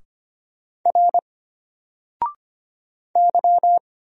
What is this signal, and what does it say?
Morse code, heard as a keyed tone, about 25 wpm, 700 Hz. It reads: R Y
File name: Single-Letter-Number-25wpm